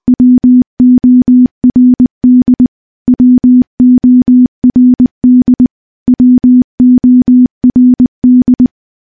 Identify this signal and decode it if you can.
Morse code, heard as a keyed tone, about 20 wpm, 250 Hz. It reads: WORD WORD WORD
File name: morse-training-20WPM